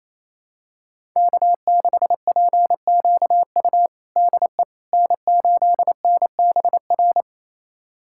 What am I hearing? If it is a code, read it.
Morse code, heard as a keyed tone, about 28 wpm, 700 Hz. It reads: K6PQU DE N8NBR
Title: USA Callsign DE Callsign - 5 Characters - 28wpm